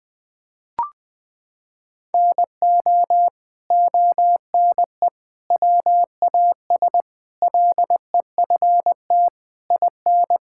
Morse code, heard as a keyed tone, about 20 wpm, 700 Hz. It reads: NO ONE WAS LEFT IN
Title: Sentences from Top 1000 Words - Review Part 5 - 20wpm